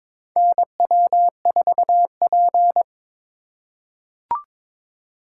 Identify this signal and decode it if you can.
Morse code, heard as a keyed tone, about 22 wpm, 700 Hz. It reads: NW4P
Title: USA Call Signs - 4 Characters - 3x Character Spacing - 22wpm